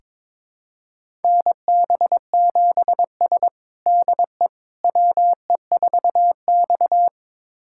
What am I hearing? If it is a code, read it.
Morse code, heard as a keyed tone, about 21 wpm, 700 Hz. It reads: NB7S DE WE4X
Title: USA Callsign DE Callsign - 4 Characters - 22wpm